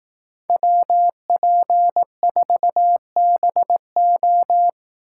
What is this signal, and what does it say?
Morse code, heard as a keyed tone, about 18 wpm, 700 Hz. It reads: WP4BO
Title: Speed Racing - USA Call Signs 15wpm